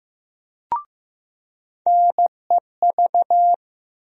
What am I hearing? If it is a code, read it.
Morse code, heard as a keyed tone, about 15 wpm, 700 Hz. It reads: NEV
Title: USA Call Sign Postfixes - 3 Letters 15wpm